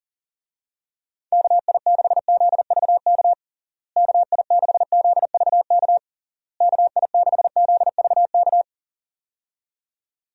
Morse code, heard as a keyed tone, about 40 wpm, 700 Hz. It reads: KI6ZVK KI6ZVK KI6ZVK
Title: USA Call Signs - 6 Characters - Repeated 3x - 40wpm